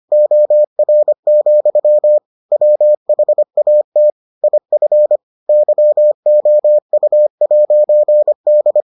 Morse code, heard as a keyed tone, about 25 wpm, 600 Hz. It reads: OR, WHAT IF YOU'D